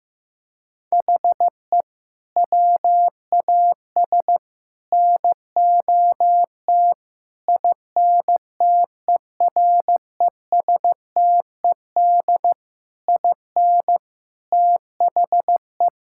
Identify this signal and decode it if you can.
Morse code, heard as a keyed tone, about 16 wpm, 700 Hz. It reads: HE WAS NOT INTERESTED IN THE
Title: Sentences from Top 1000 Words - Review Part 6 - 15wpm